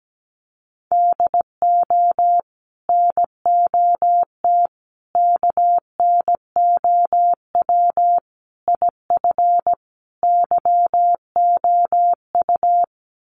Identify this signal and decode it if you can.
Morse code, heard as a keyed tone, about 17 wpm, 700 Hz. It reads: DO NOT KNOW IF YOU
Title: Sets of 5 Words 17wpm